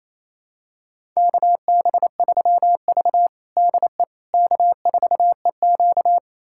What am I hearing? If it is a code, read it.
Morse code, heard as a keyed tone, about 28 wpm, 700 Hz. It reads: KB3V DE K4EQ
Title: USA Callsign DE Callsign - 4 Characters - 28wpm